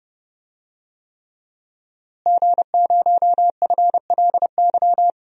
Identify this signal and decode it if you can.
Morse code, heard as a keyed tone, about 30 wpm, 700 Hz. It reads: G0FLY
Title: DXCC-100 Call Signs - 5 Characters - Repeated 3x - 30wpm